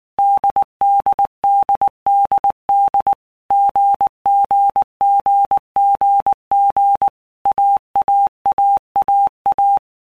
Morse code, heard as a keyed tone, about 19 wpm, 800 Hz. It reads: DDDDD GGGGG AAAAA
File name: Basic Letter Groups By Blocks / Groups@18 / AWUNDG@18